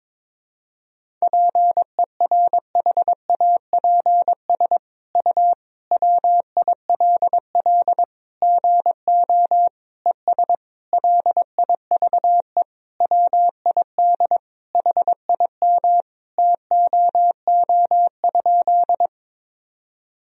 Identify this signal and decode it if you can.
Morse code, heard as a keyed tone, about 22 wpm, 700 Hz. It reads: PERHAPS U WILL GO ES LIVE WID HIM TOO?
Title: Sentences from Top 500 Words and Abbreviations 22wpm